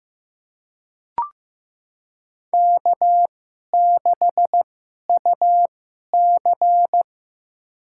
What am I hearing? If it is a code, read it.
Morse code, heard as a keyed tone, about 15 wpm, spaced as for 11 wpm, 700 Hz. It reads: K6UC
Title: USA Call Signs - 4 Characters - 2x Character Spacing - 15wpm